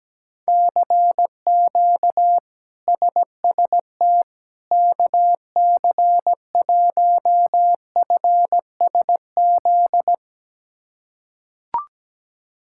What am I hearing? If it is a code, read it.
Morse code, heard as a keyed tone, about 17 wpm, 700 Hz. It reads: CQ SST KC1FSZ
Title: K1USN SST QSO Practice - 17wpm